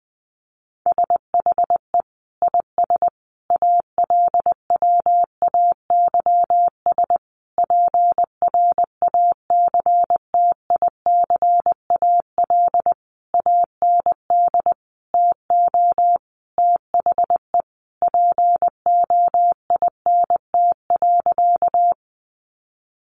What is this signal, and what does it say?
Morse code, heard as a keyed tone, about 20 wpm, 700 Hz. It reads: SHE IS ALWAYS PRACTICAL AND TO THE POINT.
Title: Sentences from Top 1400 Words 20wpm